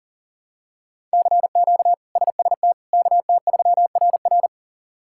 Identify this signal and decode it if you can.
Morse code, heard as a keyed tone, about 37 wpm, 700 Hz. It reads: CQ SST KT3RR
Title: K1USN SST QSO Practice - 40wpm